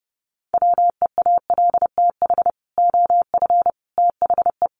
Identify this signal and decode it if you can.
Morse code, heard as a keyed tone, about 30 wpm, 700 Hz. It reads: WEALTH OF THE
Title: Sentences from Top 1200 Words 30wpm